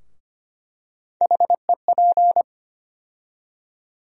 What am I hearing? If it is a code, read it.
Morse code, heard as a keyed tone, about 25 wpm, 700 Hz. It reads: HEP
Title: USA Call Sign Postfixes - 3 Letters 25wpm